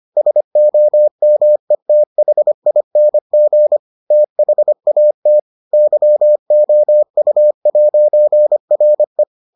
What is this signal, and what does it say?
Morse code, heard as a keyed tone, about 25 wpm, 600 Hz. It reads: SOMETHING THAT YOU'RE